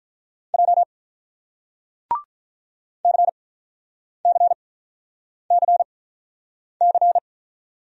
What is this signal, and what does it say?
Morse code, heard as a keyed tone, about 47 wpm, spaced as for 9 wpm, 700 Hz. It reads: J CCCC